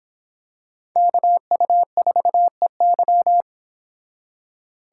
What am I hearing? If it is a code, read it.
Morse code, heard as a keyed tone, about 26 wpm, 700 Hz. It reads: KU4EY